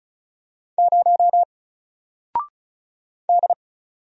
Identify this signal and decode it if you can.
Morse code, heard as a keyed tone, about 36 wpm, 700 Hz. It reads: 0 D
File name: Single-Letter-Number-35wpm